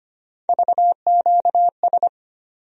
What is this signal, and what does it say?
Morse code, heard as a keyed tone, about 25 wpm, 700 Hz. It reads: VQS
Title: USA Call Sign Postfixes - 3 Letters 25wpm